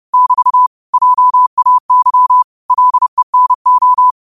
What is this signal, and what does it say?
Morse code, heard as a keyed tone, about 30 wpm, 1000 Hz. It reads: <BT> JAY LENO
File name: qotdfiles / 30wpm